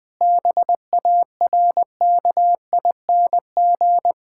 Morse code, heard as a keyed tone, about 19 wpm, 700 Hz. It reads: BARKING
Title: Pangram Challenge v2 - 20wpm